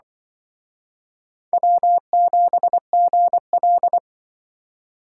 Morse code, heard as a keyed tone, about 24 wpm, 700 Hz. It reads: W7GL
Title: Speed Racing - USA 1x2 Call Signs 20wpm